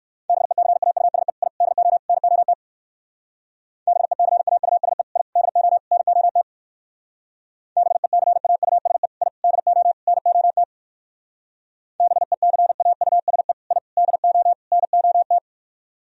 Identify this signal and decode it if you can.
Morse code, heard as a keyed tone, about 56 wpm, 700 Hz. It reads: BECAUSE I DO NOT BECAUSE I DO NOT BECAUSE I DO NOT BECAUSE I DO NOT